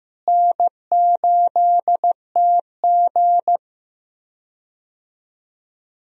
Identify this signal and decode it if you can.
Morse code, heard as a keyed tone, about 15 wpm, 700 Hz. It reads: N8TG
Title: USA Call Signs - 4 Characters - Repeated 3x - 15wpm